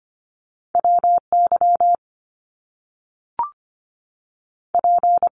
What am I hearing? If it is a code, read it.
Morse code, heard as a keyed tone, about 25 wpm, 700 Hz. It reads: WY P
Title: US States 25wpm